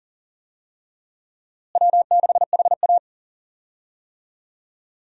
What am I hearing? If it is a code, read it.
Morse code, heard as a keyed tone, about 40 wpm, 700 Hz. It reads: W6HA
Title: USA Call Signs - 4 Characters - Repeated 3x - 40wpm